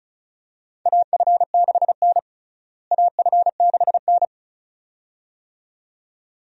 Morse code, heard as a keyed tone, about 34 wpm, 700 Hz. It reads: AF6N AF6N
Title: USA Call Signs - 4 Characters - Repeated 2x - 35wpm